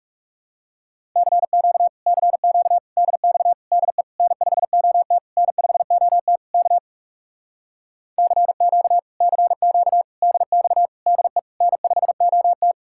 Morse code, heard as a keyed tone, about 40 wpm, 700 Hz. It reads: CQ CQ DX DE N5OT N5OT K CQ CQ DX DE N5OT